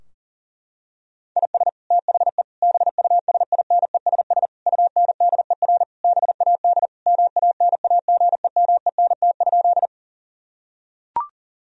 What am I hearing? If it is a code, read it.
Morse code, heard as a keyed tone, about 40 wpm, 700 Hz. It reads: IS THE BUSINESS UNDER BAD MANAGEMENT?
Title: Sentences from Top 1000 Words - Review Part 4 - 40wpm